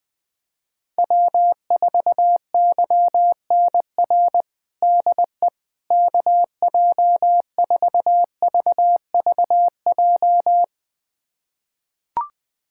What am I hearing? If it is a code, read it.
Morse code, heard as a keyed tone, about 21 wpm, 700 Hz. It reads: W4YNR DE KJ4VVJ
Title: USA Callsign DE Callsign 20wpm